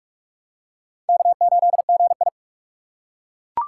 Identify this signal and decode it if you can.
Morse code, heard as a keyed tone, about 42 wpm, 700 Hz. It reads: K8GI
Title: USA Call Signs - 4 Characters - 2x Character Spacing - 45wpm